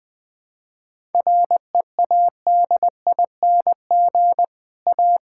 Morse code, heard as a keyed tone, about 20 wpm, 700 Hz. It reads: READING A
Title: Difficult Sentences with Hints - 20wpm